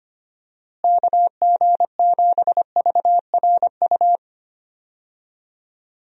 Morse code, heard as a keyed tone, about 25 wpm, 700 Hz. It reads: KG7VRU
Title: USA Call Signs - 6 Characters - Repeated 3x - 25wpm